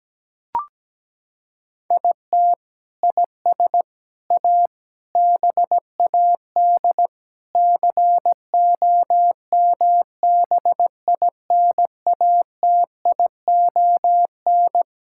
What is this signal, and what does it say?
Morse code, heard as a keyed tone, about 17 wpm, 700 Hz. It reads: IT IS A BAD COMBINATION
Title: Sentences from Top 1000 Words - Review Part 6 - 17wpm